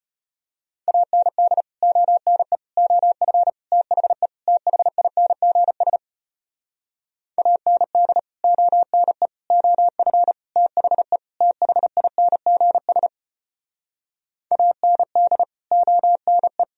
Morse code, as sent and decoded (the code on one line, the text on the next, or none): .- -. -.. / --- -. / --- ..-. / - .... / - .... .. -. --. ... / .- -. -.. / --- -. / --- ..-. / - .... / - .... .. -. --. ... / .- -. -.. / --- -. .
AND ONE OF THE THINGS AND ONE OF THE THINGS AND ONE